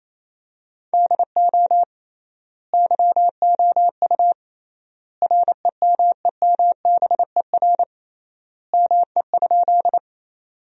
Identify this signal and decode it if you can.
Morse code, heard as a keyed tone, about 28 wpm, 700 Hz. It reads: DO YOU REMEMBER ME?
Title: 3x Word-Spacing - Sentences from Top 400 Words - 28wpm